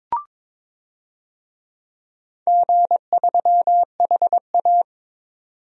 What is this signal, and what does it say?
Morse code, heard as a keyed tone, about 22 wpm, 700 Hz. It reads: G3HA
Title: DXCC-100 Call Signs - 4 Characters - 22wpm